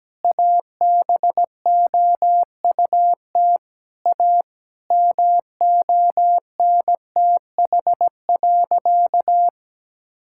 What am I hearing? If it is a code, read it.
Morse code, heard as a keyed tone, about 17 wpm, 700 Hz. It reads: ABOUT A MONTH.
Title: Sentences from Top 1000 Words - Review Part 6 - 17wpm